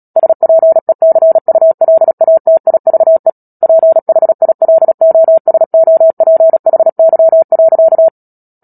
Morse code, heard as a keyed tone, about 37 wpm, 650 Hz. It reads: SPECULATIVE PHILOSOPHY.
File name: Book Two / 10-Chapter 10 - Epilogue - 36 WPM